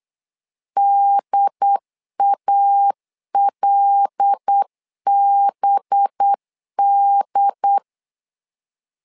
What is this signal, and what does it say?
Morse code, heard as a keyed tone, about 8 wpm, 800 Hz. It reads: DALBD
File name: LEZIONI / 5 LBVC / 5 LBVC 8wpm 2935